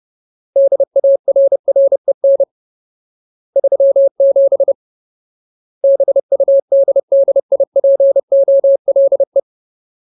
Morse code, heard as a keyed tone, about 30 wpm, 550 Hz. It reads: DARREN 37 BUDDIPOLE
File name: Ultimate Word List / 30WPM Ultimate